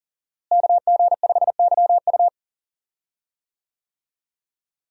Morse code, heard as a keyed tone, about 39 wpm, 700 Hz. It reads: KG5YU
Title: USA Call Signs - 5 Characters - Repeated 2x - 40wpm